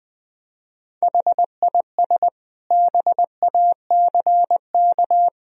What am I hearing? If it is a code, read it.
Morse code, heard as a keyed tone, about 20 wpm, 700 Hz. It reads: HIS BACK
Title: Sentences from Top 1000 Words - Review Part 1 - 20wpm